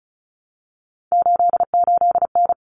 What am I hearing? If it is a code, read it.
Morse code, heard as a keyed tone, about 34 wpm, 700 Hz. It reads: 88N
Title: Common Cut Numbers - 3-Digit 35wpm